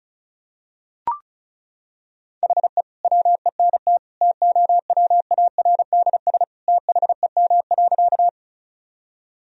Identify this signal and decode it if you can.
Morse code, heard as a keyed tone, about 36 wpm, 700 Hz. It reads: HE WENT TOWARDS THEM.
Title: Sentences from Top 1000 Words - Review Part 2 - 35wpm